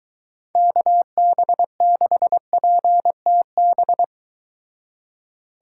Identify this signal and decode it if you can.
Morse code, heard as a keyed tone, about 22 wpm, 700 Hz. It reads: KB6PTB